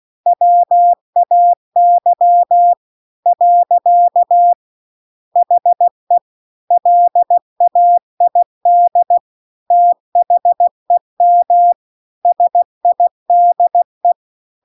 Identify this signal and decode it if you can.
Morse code, heard as a keyed tone, about 16 wpm, 700 Hz. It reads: WAY . HE LAID THEM SIDE
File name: CW-The-Big-Twoheaded-River-2-16WPM